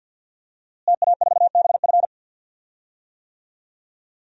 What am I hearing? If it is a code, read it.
Morse code, heard as a keyed tone, about 51 wpm, 700 Hz. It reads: TA4BF
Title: DXCC-100 Call Signs - 5 Characters - Repeated 2x - 50wpm